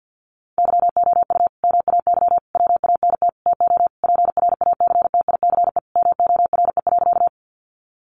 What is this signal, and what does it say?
Morse code, heard as a keyed tone, about 50 wpm, 700 Hz. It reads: YOU MAY WANT TO PRACTICE MORE.
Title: Sentences from Top 700 Words 50wpm